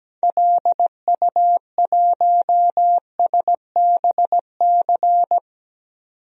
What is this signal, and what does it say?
Morse code, heard as a keyed tone, about 17 wpm, 700 Hz. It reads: LU1SBC
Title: DXCC-100 Call Signs - 6 Characters - 17wpm